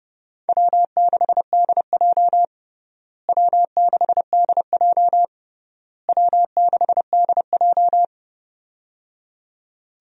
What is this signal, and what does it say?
Morse code, heard as a keyed tone, about 28 wpm, 700 Hz. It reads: W6DJ W6DJ W6DJ
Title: USA Call Signs - 4 Characters - Repeated 3x - 30wpm